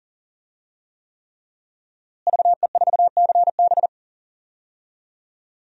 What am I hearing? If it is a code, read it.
Morse code, heard as a keyed tone, about 37 wpm, 700 Hz. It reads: VE4CB
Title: DXCC-100 Call Signs - 5 Characters - Repeated 3x - 40wpm